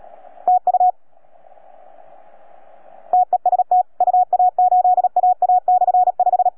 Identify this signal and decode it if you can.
Morse code, heard as a keyed tone, about 37 wpm, 720 Hz. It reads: TU TEST UA8AA/5